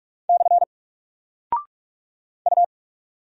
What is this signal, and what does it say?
Morse code, heard as a keyed tone, about 44 wpm, 700 Hz. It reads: / U
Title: Single Character (Letter:Number:Punctuation:Prosign) 45wpm